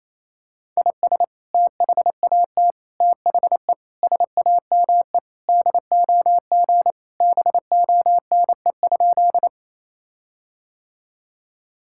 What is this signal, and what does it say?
Morse code, heard as a keyed tone, about 28 wpm, 700 Hz. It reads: IS THAT THE SAME DOG BONE?
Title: Sentences from Top 1600 Words 28wpm